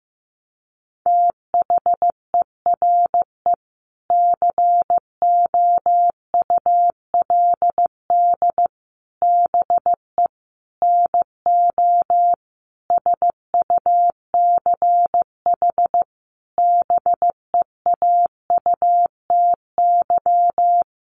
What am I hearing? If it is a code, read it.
Morse code, heard as a keyed tone, about 15 wpm, 700 Hz. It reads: THERE COULD BE NO SUCH BEAUTY
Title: Sentences from Top 600 Words 15wpm